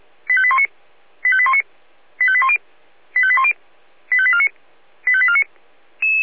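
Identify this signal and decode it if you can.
Five-tone selective call (ZVEI), heard as quick runs of five tones, tones of about 70 ms. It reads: 85819 85819 85810 85810 85849 85849
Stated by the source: ZVEI five-tone selective call received on an SDR.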